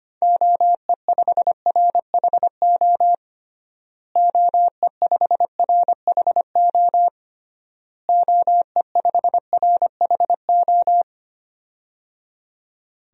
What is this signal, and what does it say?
Morse code, heard as a keyed tone, about 25 wpm, 700 Hz. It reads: OE5RHO OE5RHO OE5RHO
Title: DXCC-100 Call Signs - 6 Characters - Repeated 3x - 25wpm